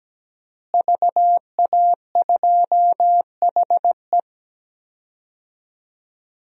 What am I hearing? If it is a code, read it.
Morse code, heard as a keyed tone, about 17 wpm, 700 Hz. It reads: VA2HE
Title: DXCC-100 Call Signs - 5 Characters - Repeated 2x - 17wpm